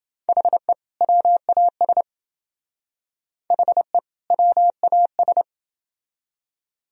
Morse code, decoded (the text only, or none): HE WAS HE WAS